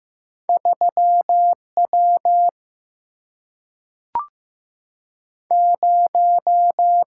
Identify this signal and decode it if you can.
Morse code, heard as a keyed tone, about 15 wpm, 700 Hz. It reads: 3 W 0
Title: Single-Letter Single-Number 15wpm